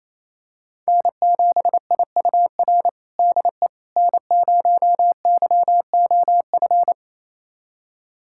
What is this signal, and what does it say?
Morse code, heard as a keyed tone, about 28 wpm, 700 Hz. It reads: N7IUR DE N0YOF
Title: USA Callsign DE Callsign - 5 Characters - 28wpm